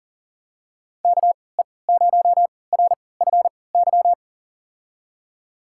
Morse code, heard as a keyed tone, about 37 wpm, spaced as for 22 wpm, 700 Hz. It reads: KE0RFY
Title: USA Call Signs - 6 Characters - 3x Character Spacing - 40wpm